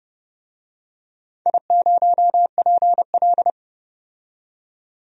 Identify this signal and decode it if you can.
Morse code, heard as a keyed tone, about 30 wpm, 700 Hz. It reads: I0PL
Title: DXCC-100 Call Signs - 4 Characters - 30wpm